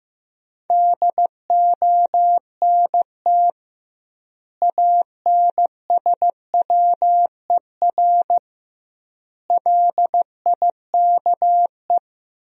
Morse code, heard as a keyed tone, about 16 wpm, 700 Hz. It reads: DONT ANSWER LIKE
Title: 2x Word-Spacing - Sentences from Top 400 Words - 15wpm